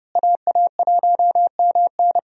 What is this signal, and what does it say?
Morse code, heard as a keyed tone, about 30 wpm, 700 Hz. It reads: AA1MN
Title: USA Call Signs - 5 Characters - Repeated 3x - 30wpm